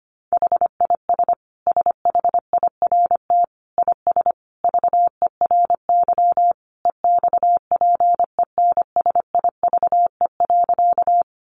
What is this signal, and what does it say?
Morse code, heard as a keyed tone, about 25 wpm, 700 Hz. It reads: HIS SHIRT IS VERY EXPENSIVE.
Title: Additional Easy Phrases 25wpm